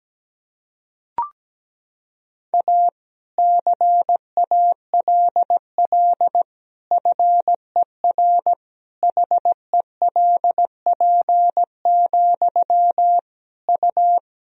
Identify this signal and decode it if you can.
Morse code, heard as a keyed tone, about 17 wpm, 700 Hz. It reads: A CALL FER HELP, U
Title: Sentences from Top 300 Words and Abbreviations 17wpm